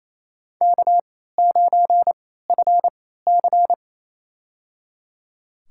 Morse code, heard as a keyed tone, about 28 wpm, spaced as for 16 wpm, 700 Hz. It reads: K9FC
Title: USA Call Signs - 4 Characters - 3x Character Spacing - 28wpm